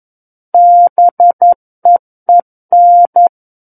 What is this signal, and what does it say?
Morse code, heard as a keyed tone, about 11 wpm, 700 Hz. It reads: BEEN
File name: Book One / 02-Chapter 2 - 11 WPM